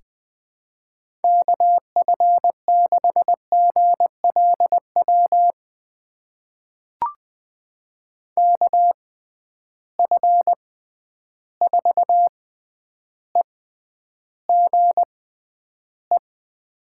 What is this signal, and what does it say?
Morse code, heard as a keyed tone, about 20 wpm, spaced as for 10 wpm, 700 Hz. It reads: KF6GLW KF4EGE
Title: USA Call Signs - 6 Characters - 6x Character Spacing - 20wpm